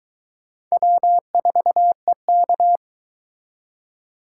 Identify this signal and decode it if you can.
Morse code, heard as a keyed tone, about 22 wpm, 700 Hz. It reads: W4EK